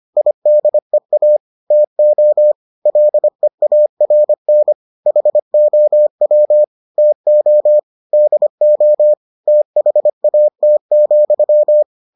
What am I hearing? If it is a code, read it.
Morse code, heard as a keyed tone, about 25 wpm, 600 Hz. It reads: IDEA TO LEARN HOW TO DO THAT,